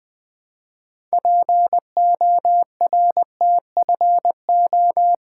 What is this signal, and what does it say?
Morse code, heard as a keyed tone, about 20 wpm, 700 Hz. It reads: PORTFO
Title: Build A Word - 3 Letters At A Time - 9-Letter Words - 20wpm